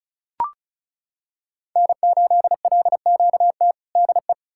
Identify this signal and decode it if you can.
Morse code, heard as a keyed tone, about 36 wpm, 700 Hz. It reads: N8LQT DE
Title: K1USN SST QSO Practice 35wpm